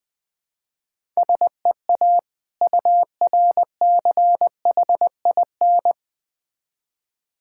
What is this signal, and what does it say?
Morse code, heard as a keyed tone, about 21 wpm, 690 Hz. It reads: SEA URCHIN
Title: Difficult Words with Hints - 20wpm